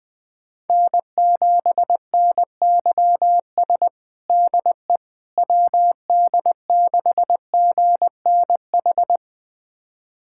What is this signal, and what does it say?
Morse code, heard as a keyed tone, about 21 wpm, 700 Hz. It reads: N7NYS DE WD6GNH
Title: USA Callsign DE Callsign 20wpm